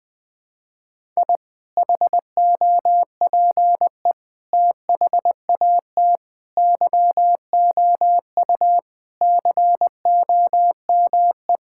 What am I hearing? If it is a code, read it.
Morse code, heard as a keyed tone, about 20 wpm, 700 Hz. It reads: I HOPE THAT YOU COME
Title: Sentences with 2-Word Phrases - Spoken Beginning - 20wpm